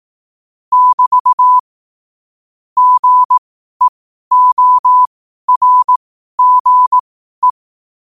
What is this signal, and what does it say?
Morse code, heard as a keyed tone, about 18 wpm, spaced as for 12 wpm, 1000 Hz. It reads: <BT> GEORGE
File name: qotdfiles / 12wpm